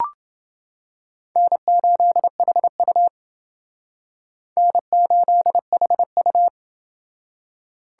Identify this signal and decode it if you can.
Morse code, heard as a keyed tone, about 28 wpm, 700 Hz. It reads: N8HU N8HU